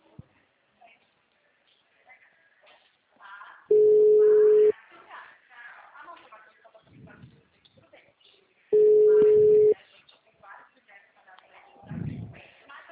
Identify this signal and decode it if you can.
Ringback tone (Europe, 425 Hz), with speech.